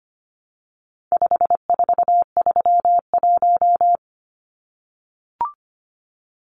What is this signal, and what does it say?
Morse code, heard as a keyed tone, about 25 wpm, 700 Hz. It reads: 5431
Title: Numbers - 4-Digit 25wpm